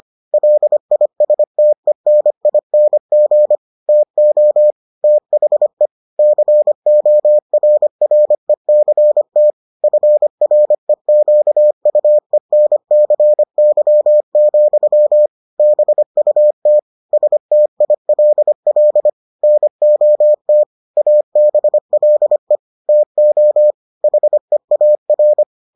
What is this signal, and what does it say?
Morse code, heard as a keyed tone, about 25 wpm, 600 Hz. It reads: LISTENING TO THE CORRECT FREQUENCY, BUT STILL NOT ABLE TO HEAR